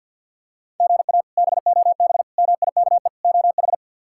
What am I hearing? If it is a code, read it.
Morse code, heard as a keyed tone, about 47 wpm, 700 Hz. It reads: GA BOB MIKE OH
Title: K1USN SST QSO Practice - 50wpm